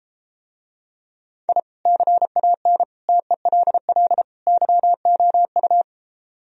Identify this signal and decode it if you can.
Morse code, heard as a keyed tone, about 33 wpm, 700 Hz. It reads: I CAN TELL YOU